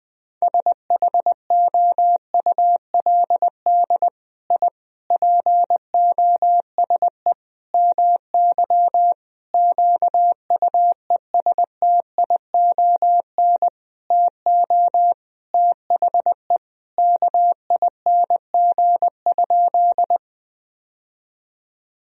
Morse code, heard as a keyed tone, about 21 wpm, 700 Hz. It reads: SHOULD I POSE MY QUESTION TO THE KING?
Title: Sentences from Top 1000 Words - Review Part 2 - 20wpm